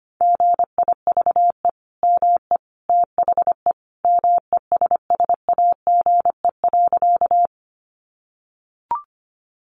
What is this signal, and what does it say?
Morse code, heard as a keyed tone, about 25 wpm, 700 Hz. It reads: GIVE ME THE MESSAGE.
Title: Sentences from Top 1100 Words 25wpm